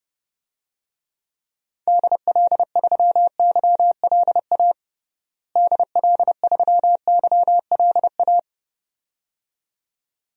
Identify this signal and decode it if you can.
Morse code, heard as a keyed tone, about 30 wpm, 700 Hz. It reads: DL3YLA DL3YLA
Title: DXCC-100 Call Signs - 6 Characters - Repeated 2x - 30wpm